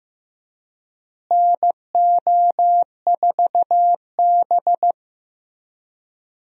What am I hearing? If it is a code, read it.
Morse code, heard as a keyed tone, about 15 wpm, 700 Hz. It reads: NO4B
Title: USA Call Signs - 4 Characters - Repeated 2x - 15wpm